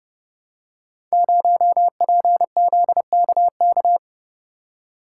Morse code, heard as a keyed tone, about 30 wpm, 700 Hz. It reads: 0PZKK
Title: Code Groups - 5 Letter:Numbers - 30wpm